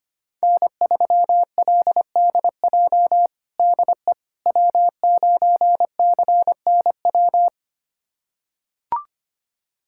Morse code, heard as a keyed tone, about 25 wpm, 700 Hz. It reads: N3LDJ DE W9CNW
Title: USA Callsign DE Callsign - 5 Characters - 25wpm